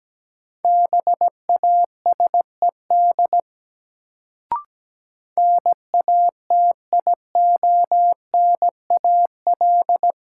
Morse code, heard as a keyed tone, about 17 wpm, 700 Hz. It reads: BASED NATIONAL
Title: Sets of 2 Words - New Word Drill - 17wpm